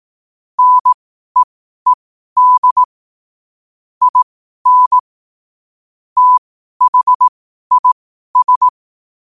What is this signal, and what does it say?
Morse code, heard as a keyed tone, about 18 wpm, spaced as for 12 wpm, 1000 Hz. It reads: NEED IN THIS
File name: qotdfiles / 12wpm